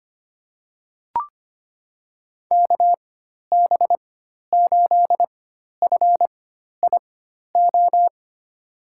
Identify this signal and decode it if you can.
Morse code, heard as a keyed tone, about 25 wpm, spaced as for 12 wpm, 700 Hz. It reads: KB8FIO
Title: USA Call Signs - 6 Characters - 4x Character Spacing - 25wpm